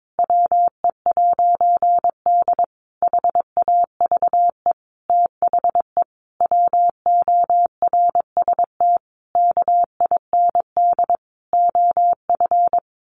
Morse code, heard as a keyed tone, about 22 wpm, 700 Hz. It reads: WE'D HAVE THE WORST KIND OF
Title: A Sound Of Thunder 22wpm